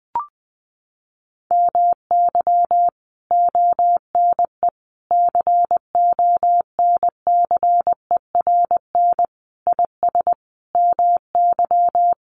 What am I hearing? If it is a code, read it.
Morse code, heard as a keyed tone, about 20 wpm, 700 Hz. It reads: MY ONE CONCERN IS MY
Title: Sentences from Top 1200 Words 20wpm